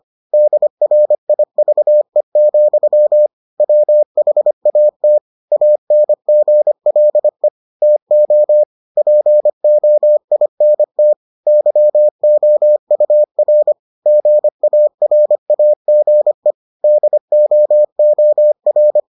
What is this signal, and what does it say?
Morse code, heard as a keyed tone, about 25 wpm, 600 Hz. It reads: DRIVE, WHAT ANGLE TO POINT YOUR GARAGE DOOR